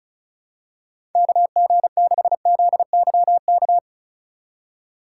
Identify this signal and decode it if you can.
Morse code, heard as a keyed tone, about 36 wpm, 700 Hz. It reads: KG6ZYK
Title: USA Call Signs - 6 Characters - 3x Character Spacing - 35wpm